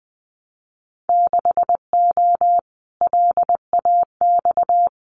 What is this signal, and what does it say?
Morse code, heard as a keyed tone, about 20 wpm, 700 Hz. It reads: 6O LAX
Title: Winter Field Day Exchanges Encore 20wpm